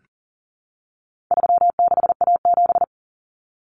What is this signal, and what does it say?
Morse code, heard as a keyed tone, about 40 wpm, 700 Hz. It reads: 36A7
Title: Common Cut Numbers - 4-Digit 40wpm